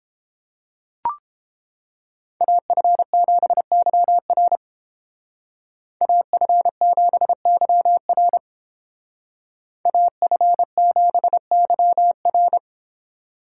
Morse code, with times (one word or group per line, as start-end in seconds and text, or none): 2.4-4.6 AF7YR
6.0-8.4 AF7YR
9.9-12.6 AF7YR